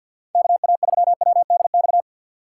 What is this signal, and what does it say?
Morse code, heard as a keyed tone, about 64 wpm, 700 Hz. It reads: KA3WDX